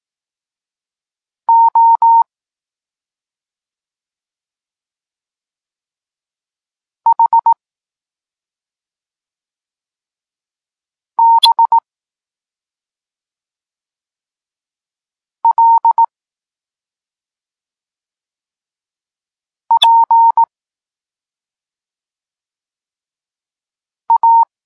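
Morse code, heard as a keyed tone, about 18 wpm, 920 Hz. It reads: O H B L P A
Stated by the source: Morse code lesson for learning the letters K and X, from the SP9MOA Niepołomicki Klub Krótkofalowców course.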